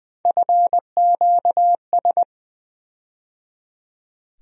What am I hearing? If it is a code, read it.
Morse code, heard as a keyed tone, about 20 wpm, 700 Hz. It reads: FQS
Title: USA Call Sign Postfixes - 3 Letters 20wpm